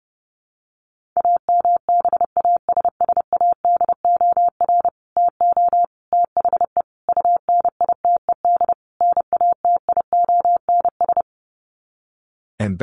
Morse code, heard as a keyed tone, about 30 wpm, 700 Hz. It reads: AMBASSADOR TO THE UNITED NATIONS
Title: Sets of 5 Words Encore2 30wpm